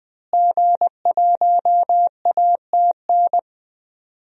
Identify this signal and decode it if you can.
Morse code, heard as a keyed tone, about 20 wpm, 700 Hz. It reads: G1ATN
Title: DXCC-100 Call Signs - 5 Characters - Repeated 3x - 20wpm